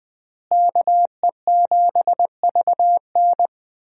Morse code, heard as a keyed tone, about 20 wpm, 700 Hz. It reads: KE7VN
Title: USA Call Signs - 5 Characters - Repeated 2x - 20wpm